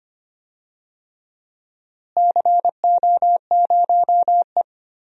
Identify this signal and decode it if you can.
Morse code, heard as a keyed tone, about 25 wpm, 700 Hz. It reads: CO0E
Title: DXCC-100 Call Signs - 4 Characters - Repeated 3x - 25wpm